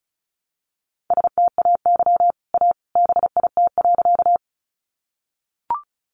Morse code, heard as a keyed tone, about 34 wpm, 700 Hz. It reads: STAY A BIT.
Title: Sentences from Top 900 Words 35wpm